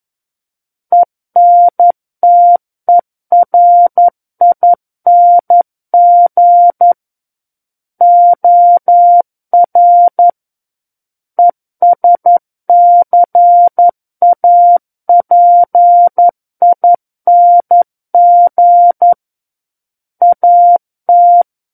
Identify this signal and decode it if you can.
Morse code, heard as a keyed tone, about 11 wpm, 700 Hz. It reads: ENTERING OR ESCAPING AT
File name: Book One / 02-Chapter 2 - 11 WPM